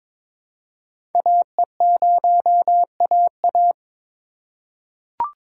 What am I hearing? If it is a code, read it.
Morse code, heard as a keyed tone, about 22 wpm, 700 Hz. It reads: AE0AA
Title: USA Call Signs - 5 Characters - 4x Character Spacing - 22wpm